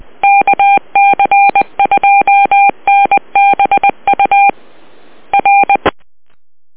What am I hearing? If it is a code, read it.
Morse code, heard as a keyed tone, about 20 wpm, 800 Hz. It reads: KC2NBU R
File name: net audio